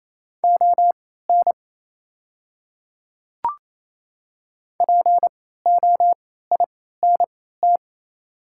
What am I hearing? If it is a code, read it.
Morse code, heard as a keyed tone, about 27 wpm, spaced as for 16 wpm, 700 Hz. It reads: ON POINT
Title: 3x Character Spacing - Top 200 Words - 28wpm